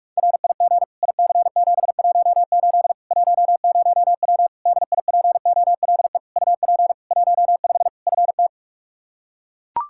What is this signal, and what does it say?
Morse code, heard as a keyed tone, about 41 wpm, 700 Hz. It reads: RIG IC718 10W DIPOLE UP 15 FT
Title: Practice QSO - Exchange 2 - Info - 45wpm